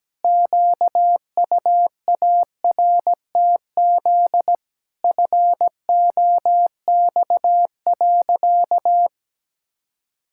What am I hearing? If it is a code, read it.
Morse code, heard as a keyed tone, about 17 wpm, 700 Hz. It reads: QUARTZ FOX.
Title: Pangram Challenge v2 - 17wpm